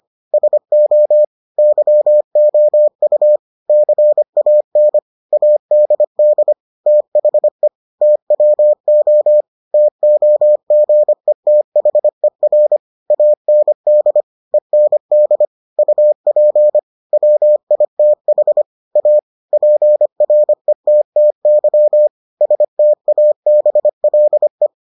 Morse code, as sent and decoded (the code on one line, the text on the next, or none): ... --- / -.-- --- ..- / -.-. .- -. / .- -.. -.. / - .... / - .-- --- / - --- --. - .... .-. / .- -. -.. / . -. -.. / ..- .--. / .-- .. - .... / .- / .--. .-. - - -.-- / ... - .- -... .-.. .
SO YOU CAN ADD THE TWO TOGETHER AND END UP WITH A PRETTY STABLE